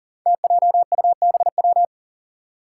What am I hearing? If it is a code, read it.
Morse code, heard as a keyed tone, about 38 wpm, 700 Hz. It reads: TJUBW
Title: Five Challenging Letter-Number Combinations 40wpm